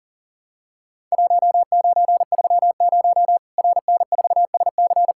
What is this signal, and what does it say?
Morse code, heard as a keyed tone, about 40 wpm, 700 Hz. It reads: 1930 RN4SC
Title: Worked All Europe DX Contest - QTC Exchange - 40wpm